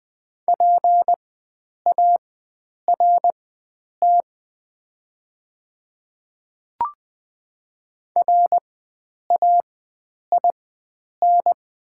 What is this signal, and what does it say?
Morse code, heard as a keyed tone, about 20 wpm, spaced as for 9 wpm, 690 Hz. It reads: PART RAIN
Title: 4x Character Spacing - Top 500 Words - 20wpm